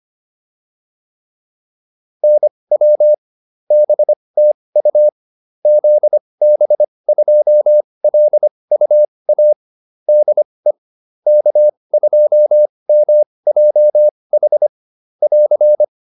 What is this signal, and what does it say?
Morse code, heard as a keyed tone, about 25 wpm, spaced as for 20 wpm, 600 Hz. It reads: NW BTU ZB2LUA DE K2MJH <AR>